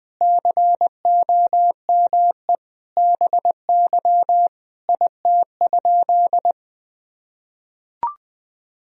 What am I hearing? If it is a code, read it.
Morse code, heard as a keyed tone, about 21 wpm, 700 Hz. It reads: COME BY IT?
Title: Sentences from Top 100 Words and Abbreviations 20wpm